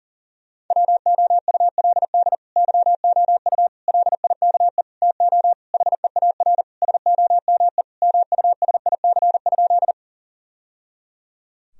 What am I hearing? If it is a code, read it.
Morse code, heard as a keyed tone, about 40 wpm, 700 Hz. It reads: WOULD YOU LIKE TO HEAR SOME MUSIC?
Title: Sentences from Top 1000 Words - Review Part 1 - 40wpm